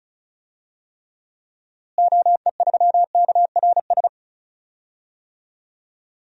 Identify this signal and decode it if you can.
Morse code, heard as a keyed tone, about 34 wpm, 700 Hz. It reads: OE3KRS
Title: DXCC-100 Call Signs - 6 Characters - Repeated 2x - 35wpm